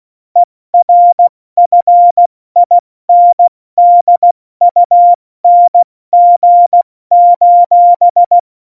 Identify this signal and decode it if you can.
Morse code, heard as a keyed tone, about 16 wpm, 700 Hz. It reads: ERFINDUNG:
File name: Morsekurs / Witze / Witz 26. Januar 2021, 70 F80